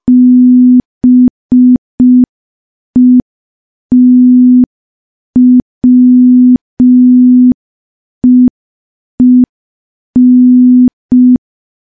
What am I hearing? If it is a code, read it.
Morse code, heard as a keyed tone, about 5 wpm, 250 Hz. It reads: BETWEEN
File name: morse / morse-training-05WPM